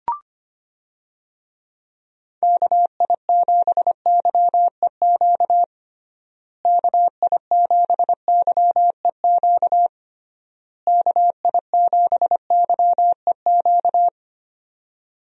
Morse code, heard as a keyed tone, about 25 wpm, 700 Hz. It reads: KI7YEQ KI7YEQ KI7YEQ
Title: USA Call Signs - 6 Characters - Repeated 3x - 25wpm